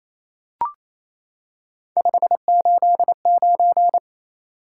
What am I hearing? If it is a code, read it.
Morse code, heard as a keyed tone, about 28 wpm, 700 Hz. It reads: 589
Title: Top 180 QSO Elements 28wpm